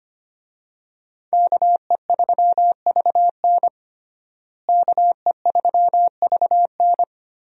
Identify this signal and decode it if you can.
Morse code, heard as a keyed tone, about 25 wpm, 700 Hz. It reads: KE3VN KE3VN
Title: USA Call Signs - 5 Characters - Repeated 3x - 25wpm